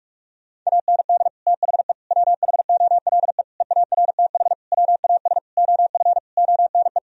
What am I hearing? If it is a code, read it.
Morse code, heard as a keyed tone, about 46 wpm, 700 Hz. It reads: AND THE WHOLE EARTH WAS OF ONE
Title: Sentences from Top 1000 Words - Review Part 3 - 45wpm